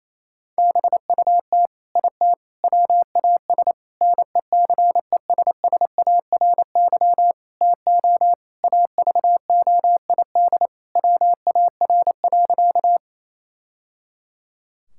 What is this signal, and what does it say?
Morse code, heard as a keyed tone, about 28 wpm, 700 Hz. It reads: BUT IT WAS NECESSARY TO AVOID WAR.
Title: Sentences from Top 1000 Words - Review Part 6 - 28wpm